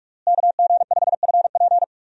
Morse code, heard as a keyed tone, about 41 wpm, 700 Hz. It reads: KG5FP
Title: USA Call Signs - 5 Characters - Repeated 3x - 45wpm